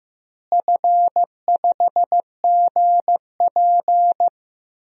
Morse code, heard as a keyed tone, about 15 wpm, 700 Hz. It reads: F5GP
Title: DXCC-100 Call Signs - 4 Characters - Repeated 2x - 15wpm